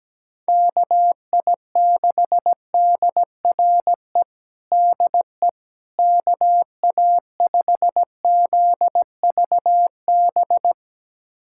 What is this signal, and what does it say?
Morse code, heard as a keyed tone, about 17 wpm, 700 Hz. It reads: KI6DRE DE KA5ZVB
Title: USA Callsign DE Callsign 17wpm